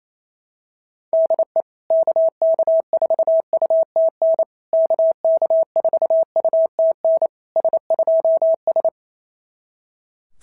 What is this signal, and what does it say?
Morse code, heard as a keyed tone, about 28 wpm, 690 Hz. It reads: DE KK4UTN KK4UTN S2S
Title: SOTA QSO Practice 28wpm